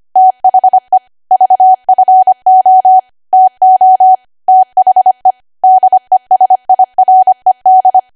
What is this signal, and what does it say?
Morse code, heard as a keyed tone, about 25 wpm, 750 Hz. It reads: THE VFO TO THE DESIRED
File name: Archive / 25 WPM / 25WPM